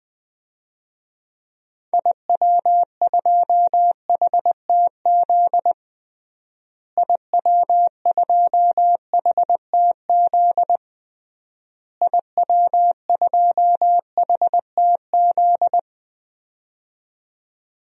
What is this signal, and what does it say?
Morse code, heard as a keyed tone, about 20 wpm, 700 Hz. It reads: IW2HTZ IW2HTZ IW2HTZ
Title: DXCC-100 Call Signs - 6 Characters - Repeated 3x - 20wpm